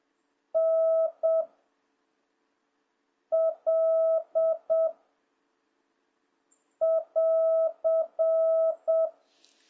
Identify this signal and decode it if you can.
Morse code, heard as a keyed tone, about 7 wpm, 660 Hz. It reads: N L <AR>